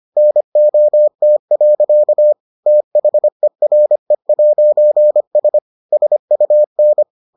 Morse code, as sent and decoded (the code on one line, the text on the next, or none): -. --- - .-.-.- / - .... .-. .----. ... / ... ..- -.
NOT. THERE'S SUN